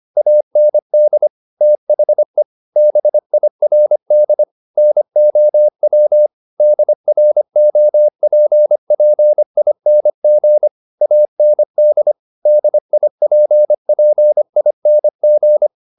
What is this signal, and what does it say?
Morse code, heard as a keyed tone, about 25 wpm, 600 Hz. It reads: AND THE BIRD NOW DROPPING AND DIPPING
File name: the old man and the sea part 1 25wpm